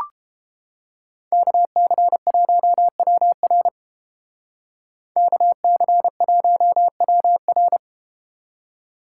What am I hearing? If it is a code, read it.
Morse code, heard as a keyed tone, about 32 wpm, 700 Hz. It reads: KC1WR KC1WR